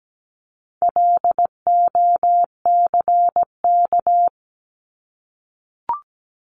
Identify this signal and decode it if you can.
Morse code, heard as a keyed tone, about 17 wpm, 700 Hz. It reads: LOCK
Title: Top 1001-1250 Words 17wpm